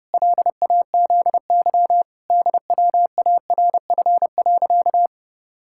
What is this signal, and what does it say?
Morse code, heard as a keyed tone, about 30 wpm, 700 Hz. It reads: LAZY DWARF.
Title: Pangram Challenge v2 - 30wpm